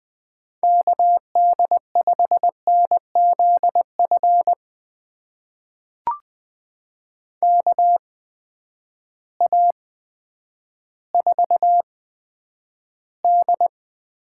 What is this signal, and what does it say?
Morse code, heard as a keyed tone, about 20 wpm, spaced as for 10 wpm, 700 Hz. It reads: KD5NZF KA4D
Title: USA Call Signs - 8x Character Spacing - 20wpm